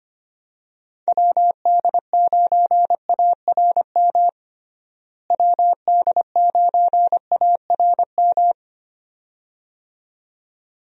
Morse code, heard as a keyed tone, about 25 wpm, 700 Hz. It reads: WD9ARM WD9ARM
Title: USA Call Signs - 6 Characters - Repeated 2x - 25wpm